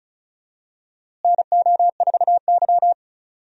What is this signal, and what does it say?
Morse code, heard as a keyed tone, about 36 wpm, 700 Hz. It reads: NO4Y
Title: USA Call Signs - 4 Characters - Repeated 2x - 35wpm